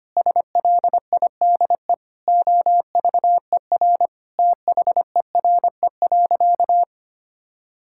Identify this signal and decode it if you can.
Morse code, heard as a keyed tone, about 25 wpm, 700 Hz. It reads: SLIDE OVER THERE.
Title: Sentences from Top 1600 Words 25wpm